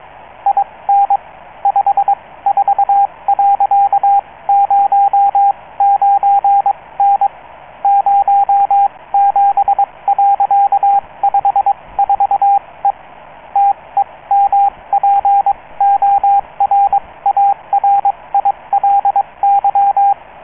Morse code, heard as a keyed tone, about 22 wpm, spaced as for 18 wpm, 800 Hz, with radio noise. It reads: IN 54.09N 07.54E TEMPORARILY